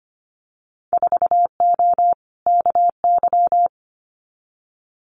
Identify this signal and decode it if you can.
Morse code, heard as a keyed tone, about 25 wpm, 700 Hz. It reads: 4O KY
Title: Winter Field Day Exchanges Encore 25wpm